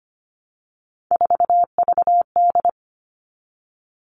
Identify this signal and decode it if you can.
Morse code, heard as a keyed tone, about 25 wpm, 700 Hz. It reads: 4VD
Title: All Cut Numbers - 3-Digit 25wpm